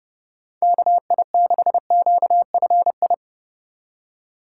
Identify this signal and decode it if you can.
Morse code, heard as a keyed tone, about 35 wpm, 700 Hz. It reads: KI6QFI
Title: Speed Racing - USA 2x3 Call Signs 30wpm